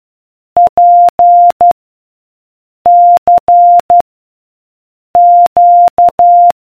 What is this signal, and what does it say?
Morse code, heard as a keyed tone, about 12 wpm, 700 Hz. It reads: P C Q